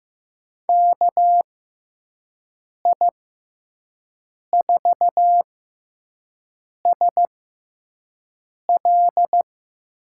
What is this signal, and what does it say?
Morse code, heard as a keyed tone, about 15 wpm, spaced as for 5 wpm, 700 Hz. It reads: KI4SL
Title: USA Call Signs - 6 Characters - 6x Character Spacing - 15wpm